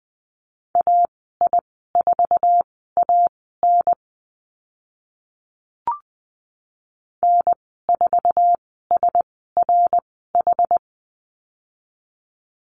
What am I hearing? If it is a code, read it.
Morse code, heard as a keyed tone, about 20 wpm, spaced as for 15 wpm, 700 Hz. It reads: AI4AN N4SRH
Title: 076 Lesson 13g - Letter L - Call Signs 20wpm